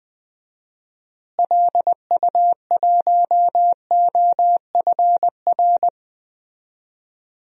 Morse code, heard as a keyed tone, about 20 wpm, 700 Hz. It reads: LU1OFR
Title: DXCC-100 Call Signs - 6 Characters - Repeated 3x - 20wpm